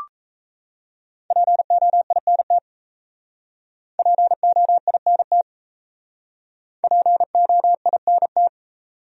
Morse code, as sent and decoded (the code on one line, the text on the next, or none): .--. --- .. -. - / .--. --- .. -. - / .--. --- .. -. -
POINT POINT POINT